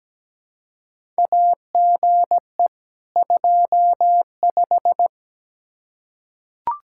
Morse code, heard as a keyed tone, about 17 wpm, 700 Hz. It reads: AGE 25
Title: Practice QSO - Exchange 3 - More Info - 17wpm